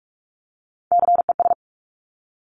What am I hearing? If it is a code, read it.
Morse code, heard as a keyed tone, about 46 wpm, 700 Hz. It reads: CES
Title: Top Three Letter Combinations in Words 45wpm